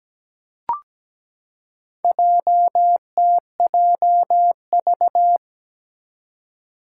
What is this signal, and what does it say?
Morse code, heard as a keyed tone, about 17 wpm, 700 Hz. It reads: JTJV
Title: Four Challenging Letter-Number Combinations 17wpm